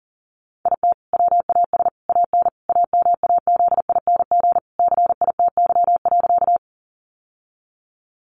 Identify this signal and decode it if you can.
Morse code, heard as a keyed tone, about 40 wpm, 700 Hz. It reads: IT WAS AN AMAZING CITY.
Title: Sentences from Top 1500 Words 40wpm